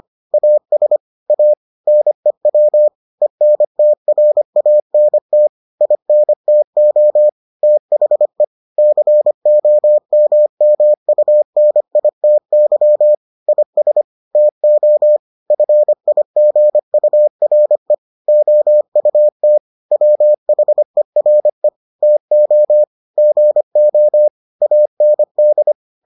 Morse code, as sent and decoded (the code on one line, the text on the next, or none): .- ... / .- / -. .-- / . -. - .-. .- -. - / .. -. - --- / - .... / -.-. --- -- -- ..- -. .. - -.-- / .. ... / - --- / ..-. .. --. ..- .-. / --- ..- - / .-- .... .-. / - --- / --. --- / .- -. -..
AS A NEW ENTRANT INTO THE COMMUNITY IS TO FIGURE OUT WHERE TO GO AND